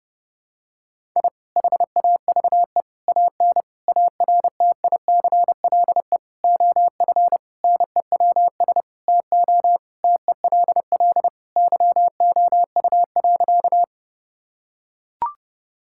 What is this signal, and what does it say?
Morse code, heard as a keyed tone, about 30 wpm, 700 Hz. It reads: I HAVE AN ARTICLE OF NEWS TO TELL YOU.
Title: Sentences from Top 1000 Words - Review Part 5 - 30wpm